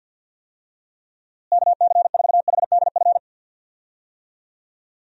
Morse code, heard as a keyed tone, about 51 wpm, 700 Hz. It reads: KC4HDF
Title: USA Call Signs - 6 Characters - Repeated 3x - 50wpm